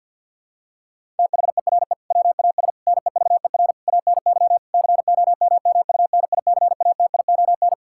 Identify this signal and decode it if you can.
Morse code, heard as a keyed tone, about 50 wpm, 700 Hz. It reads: THERE WAS NEVER ANY COMMUNICATION
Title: Sentences from Top 1000 Words - Review Part 6 - 50wpm